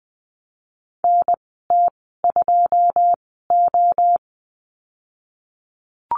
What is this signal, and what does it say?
Morse code, heard as a keyed tone, about 20 wpm, spaced as for 14 wpm, 700 Hz. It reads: NT2O
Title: 122 Lesson 20g - Number 3 - Call Signs 20wpm